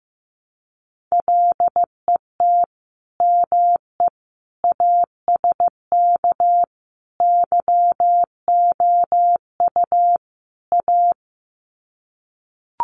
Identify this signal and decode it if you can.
Morse code, heard as a keyed tone, about 15 wpm, 700 Hz. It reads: LET ME ASK YOU A
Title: Sets of 5 Words Encore 15wpm